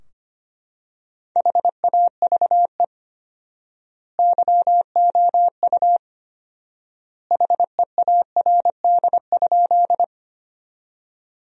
Morse code, heard as a keyed tone, about 25 wpm, 700 Hz. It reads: HAVE YOU HEARD?
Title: 4x Word-Spacing - Sentences from Top 400 Words - 25wpm